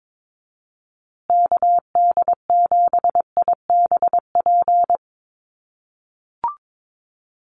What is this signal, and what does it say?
Morse code, heard as a keyed tone, about 22 wpm, 700 Hz. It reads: KD7IBP
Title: USA Call Signs Repeated 3x 22wpm